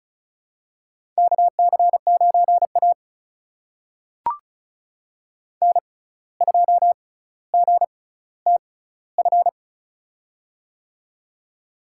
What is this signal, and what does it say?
Morse code, heard as a keyed tone, about 36 wpm, spaced as for 17 wpm, 700 Hz. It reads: KC9A N2GTF
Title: USA Call Signs - 6x Character Spacing - 35wpm